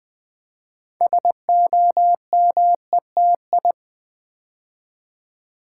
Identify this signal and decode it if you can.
Morse code, heard as a keyed tone, about 20 wpm, 700 Hz. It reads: SOMETI
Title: Build A Word - 3 Letters At A Time - 9-Letter Words - 20wpm